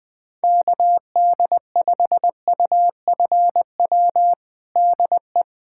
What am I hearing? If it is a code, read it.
Morse code, heard as a keyed tone, about 20 wpm, 700 Hz. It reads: KD5UFW DE
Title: USA Callsign DE Callsign - 6 Characters - 20wpm